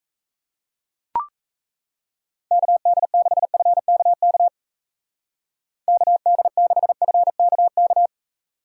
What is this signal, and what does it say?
Morse code, heard as a keyed tone, about 35 wpm, 700 Hz. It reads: KD6FKK KD6FKK
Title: Speed Racing - USA 2x3 Call Signs 28wpm